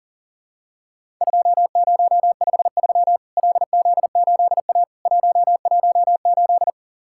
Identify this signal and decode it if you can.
Morse code, heard as a keyed tone, about 40 wpm, 700 Hz. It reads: 2053 LZ8A 118
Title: Worked All Europe DX Contest - QTC Exchange - 40wpm